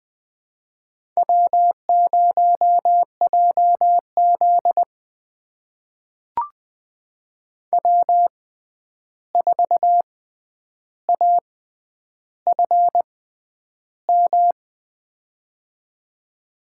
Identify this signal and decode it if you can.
Morse code, heard as a keyed tone, about 20 wpm, spaced as for 10 wpm, 700 Hz. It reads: W0JZ W4AFM
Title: USA Call Signs - 6x Character Spacing - 20wpm